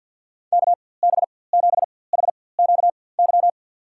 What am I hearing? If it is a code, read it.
Morse code, heard as a keyed tone, about 47 wpm, spaced as for 23 wpm, 700 Hz. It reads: KB7HQY
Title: USA Call Signs - 4x Character Spacing - 50wpm